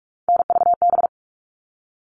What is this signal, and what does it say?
Morse code, heard as a keyed tone, about 44 wpm, 700 Hz. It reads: NVB
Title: All Cut Numbers - 3-Digit 45wpm